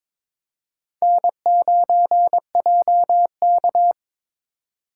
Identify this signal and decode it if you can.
Morse code, heard as a keyed tone, about 22 wpm, 700 Hz. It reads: N9JK
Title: USA Call Signs - 4 Characters - 2x Character Spacing - 22wpm